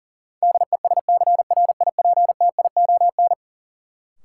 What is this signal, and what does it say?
Morse code, heard as a keyed tone, about 37 wpm, 700 Hz. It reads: DESCRIPTION
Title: Sentences from Top 1300 Words - New Word Drill - 40wpm